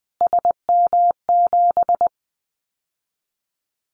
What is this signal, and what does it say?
Morse code, heard as a keyed tone, about 20 wpm, 700 Hz. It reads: SM7
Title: Callsign Prefixes for Top 25 DXCC - Repeated 3x - 20wpm